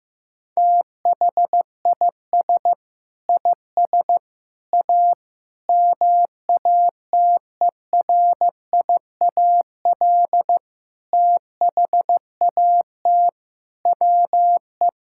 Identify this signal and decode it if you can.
Morse code, heard as a keyed tone, about 15 wpm, 700 Hz. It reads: THIS IS A MATERIAL THAT WE
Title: Sentences from Top 1000 Words - Review Part 5 - 15wpm